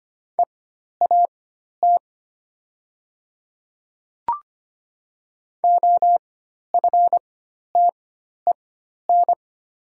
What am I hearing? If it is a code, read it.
Morse code, heard as a keyed tone, about 25 wpm, spaced as for 12 wpm, 700 Hz. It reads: EAT OFTEN
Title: 4x Character Spacing - Top 500 Words - 25wpm